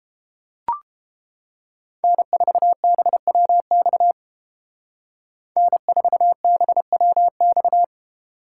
Morse code, heard as a keyed tone, about 31 wpm, 700 Hz. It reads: N4BWX N4BWX